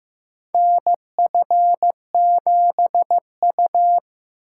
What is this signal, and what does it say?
Morse code, heard as a keyed tone, about 15 wpm, 700 Hz. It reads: NF7U
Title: USA Call Signs - 4 Characters - 2x Character Spacing - 15wpm